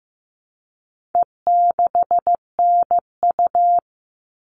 Morse code, heard as a keyed tone, about 15 wpm, 700 Hz. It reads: E6NU
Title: All Cut Numbers - 4-Digit 15wpm